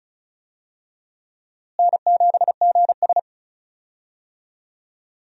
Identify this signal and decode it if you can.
Morse code, heard as a keyed tone, about 36 wpm, 700 Hz. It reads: N7GS
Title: USA Call Signs - 4 Characters - Repeated 2x - 35wpm